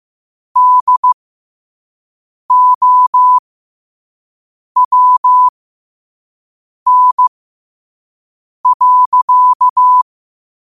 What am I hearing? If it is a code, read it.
Morse code, heard as a keyed tone, about 15 wpm, spaced as for 5 wpm, 1000 Hz. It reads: DOWN.
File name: qotdfiles / 5wpm